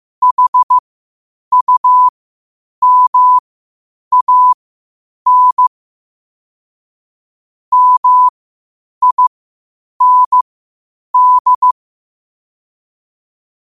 Morse code, heard as a keyed tone, about 15 wpm, spaced as for 8 wpm, 1000 Hz. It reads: HUMAN MIND
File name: qotdfiles / 8wpm